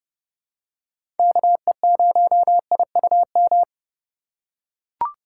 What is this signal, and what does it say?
Morse code, heard as a keyed tone, about 30 wpm, 700 Hz. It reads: KE0IUM
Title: USA Call Signs - 6 Characters - 2x Character Spacing - 30wpm